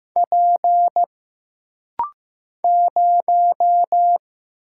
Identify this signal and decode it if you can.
Morse code, heard as a keyed tone, about 15 wpm, 700 Hz. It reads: P 0
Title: Single Letter-Number Alt - Rapid-Fire 15wpm